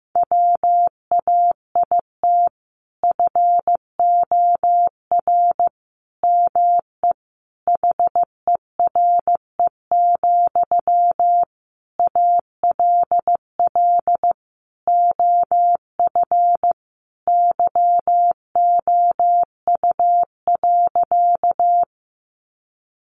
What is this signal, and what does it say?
Morse code, heard as a keyed tone, about 14 wpm, 700 Hz. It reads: WAIT FOR ME HERE, ALL OF YOU.
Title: Sentences from Top 500 Words 15wpm